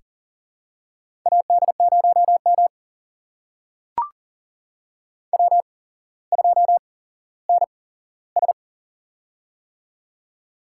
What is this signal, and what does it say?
Morse code, heard as a keyed tone, about 39 wpm, spaced as for 17 wpm, 700 Hz. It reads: AD0M W2NS
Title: USA Call Signs - 8x Character Spacing - 40wpm